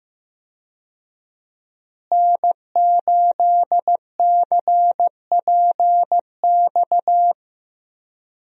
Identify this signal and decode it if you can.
Morse code, heard as a keyed tone, about 15 wpm, 700 Hz. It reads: N8CPX
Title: USA Call Signs - 5 Characters - Repeated 3x - 15wpm